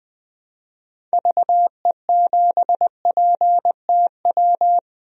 Morse code, heard as a keyed tone, about 20 wpm, 700 Hz. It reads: VE7PTW
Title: DXCC-100 Call Signs - 6 Characters - 20wpm